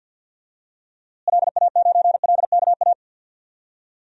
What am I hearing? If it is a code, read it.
Morse code, heard as a keyed tone, about 51 wpm, 700 Hz. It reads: LA9LKA
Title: DXCC-100 Call Signs - 6 Characters - Repeated 2x - 50wpm